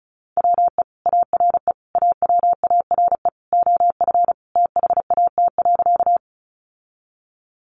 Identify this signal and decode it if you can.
Morse code, heard as a keyed tone, about 34 wpm, 700 Hz. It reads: WE ARE AWARE OF THAT.
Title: Sentences from Top 1200 Words 35wpm